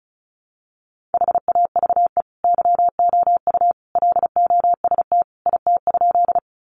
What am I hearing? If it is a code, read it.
Morse code, heard as a keyed tone, about 34 wpm, 700 Hz. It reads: HAVE YOU LOST IT?
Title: Sentences from Top 1400 Words 35wpm